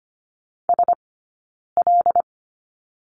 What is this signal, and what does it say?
Morse code, heard as a keyed tone, about 25 wpm, spaced as for 9 wpm, 690 Hz. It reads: SL
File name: course / 25:10wpm / 092 Lesson 16d - Letter c - Two Characters 25wpm